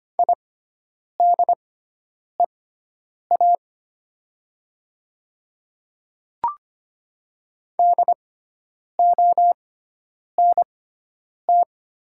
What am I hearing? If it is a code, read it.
Morse code, heard as a keyed tone, about 25 wpm, spaced as for 9 wpm, 700 Hz. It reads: IDEA DONT
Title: 6x Character Spacing - Top 300 Words - 25wpm